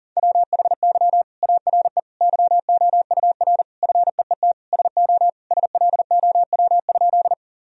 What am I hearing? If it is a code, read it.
Morse code, heard as a keyed tone, about 40 wpm, 700 Hz. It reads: WHY ARE YOUR FEET SO SLOW?
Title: Sentences from Top 1000 Words - Review Part 2 - 40wpm